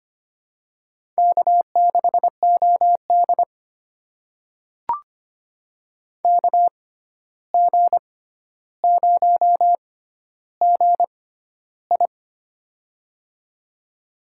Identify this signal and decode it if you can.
Morse code, heard as a keyed tone, about 25 wpm, spaced as for 12 wpm, 700 Hz. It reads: K6OD KG0GI
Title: USA Call Signs - 6x Character Spacing - 25wpm